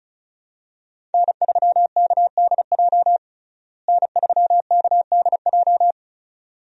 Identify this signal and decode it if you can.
Morse code, heard as a keyed tone, about 34 wpm, 700 Hz. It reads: N3KDJ N3KDJ
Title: USA Call Signs - 5 Characters - Repeated 2x - 35wpm